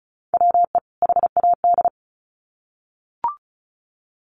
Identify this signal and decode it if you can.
Morse code, heard as a keyed tone, about 34 wpm, 700 Hz. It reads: WE HAD
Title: Sets of 2 Words 35wpm